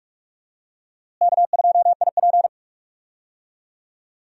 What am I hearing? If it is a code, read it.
Morse code, heard as a keyed tone, about 46 wpm, 700 Hz. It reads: K2IP
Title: USA Call Signs - 4 Characters - Repeated 3x - 45wpm